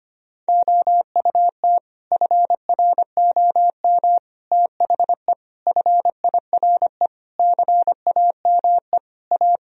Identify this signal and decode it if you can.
Morse code, heard as a keyed tone, about 25 wpm, 700 Hz. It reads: OUT FROM THE FIRE CAME A
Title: Sentences from Top 1000 Words - Review Part 3 - 25wpm